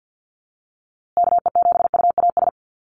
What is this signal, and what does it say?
Morse code, heard as a keyed tone, about 50 wpm, 700 Hz. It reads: KE7UAS
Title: USA Call Signs Repeated 3x 50wpm